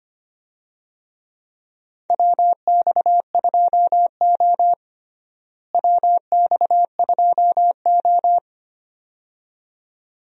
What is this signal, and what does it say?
Morse code, heard as a keyed tone, about 25 wpm, 700 Hz. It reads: WX2O WX2O
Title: USA Call Signs - 4 Characters - Repeated 2x - 25wpm